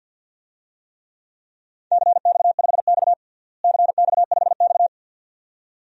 Morse code, heard as a keyed tone, about 50 wpm, 700 Hz. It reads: CX5X CX5X
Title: DXCC-100 Call Signs - 4 Characters - Repeated 2x - 50wpm